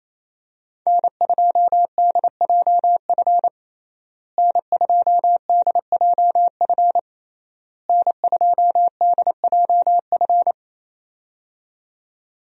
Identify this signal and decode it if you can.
Morse code, heard as a keyed tone, about 27 wpm, 700 Hz. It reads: N2DJF N2DJF N2DJF
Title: USA Call Signs - 5 Characters - Repeated 3x - 28wpm